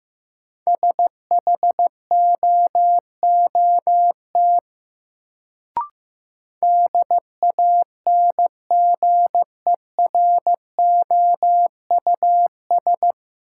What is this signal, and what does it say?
Morse code, heard as a keyed tone, about 15 wpm, 700 Hz. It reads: SHOOT DANGEROUS
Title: Sentences from Top 1200 Words - New Word Drill Alt - 15wpm